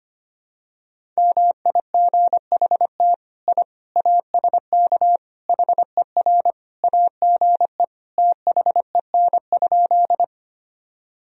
Morse code, heard as a keyed tone, about 25 wpm, 700 Hz. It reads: MIGHT I ASK HER AGE THEN?
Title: Sentences from Top 1000 Words - Review Part 3 - 25wpm